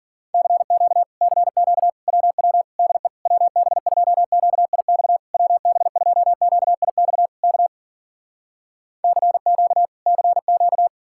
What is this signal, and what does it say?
Morse code, heard as a keyed tone, about 60 wpm, 700 Hz. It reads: CQ CQ WW DE WB2QIX WB2QIX K CQ CQ